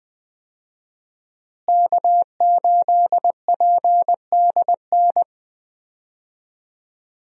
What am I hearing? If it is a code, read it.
Morse code, heard as a keyed tone, about 21 wpm, 700 Hz. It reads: K8PDN
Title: USA Call Signs - 5 Characters - Repeated 2x - 20wpm